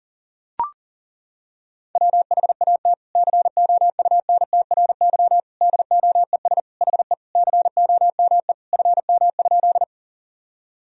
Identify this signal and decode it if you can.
Morse code, heard as a keyed tone, about 40 wpm, 700 Hz. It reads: WHAT COUNTRY DOES HE COME FM?
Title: Sentences from Top 300 Words and Abbreviations 40wpm